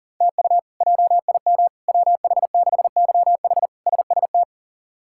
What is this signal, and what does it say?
Morse code, heard as a keyed tone, about 40 wpm, 700 Hz. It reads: TU JIM WH6YH SST
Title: K1USN SST QSO Practice - 40wpm